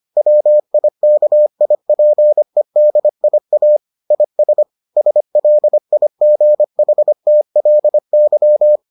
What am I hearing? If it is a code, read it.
Morse code, heard as a keyed tone, about 25 wpm, 600 Hz. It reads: WIKIPEDIA IS SLIGHTLY